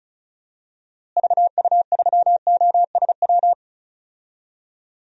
Morse code, heard as a keyed tone, about 36 wpm, 700 Hz. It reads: VU3OSW
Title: DXCC-100 Call Signs - 6 Characters - Repeated 2x - 35wpm